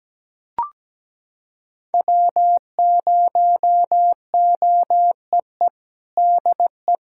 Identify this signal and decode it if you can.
Morse code, heard as a keyed tone, about 17 wpm, 700 Hz. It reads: W0OEE DE
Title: USA Callsign DE Callsign - 5 Characters - 17wpm